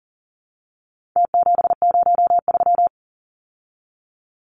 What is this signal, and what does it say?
Morse code, heard as a keyed tone, about 40 wpm, 700 Hz. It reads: T703
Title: Numbers - Contest Style - 4-Digit 40wpm